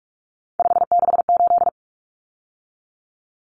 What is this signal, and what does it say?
Morse code, heard as a keyed tone, about 44 wpm, 700 Hz. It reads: 568
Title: Common Cut Numbers - 3-Digit 45wpm